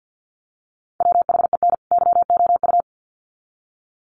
Morse code, heard as a keyed tone, about 50 wpm, 700 Hz. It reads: WHEN YOU
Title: Sets of 2 Words 50wpm